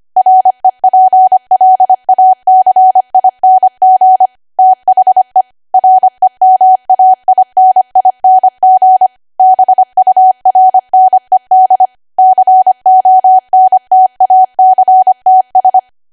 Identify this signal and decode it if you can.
Morse code, heard as a keyed tone, about 25 wpm, 750 Hz. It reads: REPLACING THE REMAINING BURNED CONTACTS
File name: Archive / 25 WPM / 25WPM